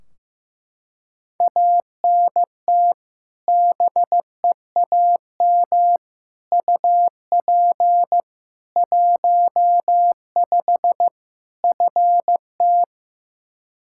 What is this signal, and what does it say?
Morse code, heard as a keyed tone, about 15 wpm, 700 Hz. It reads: ANT BEAM UP 15 FT
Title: Practice QSO - Exchange 2 - Info - 15wpm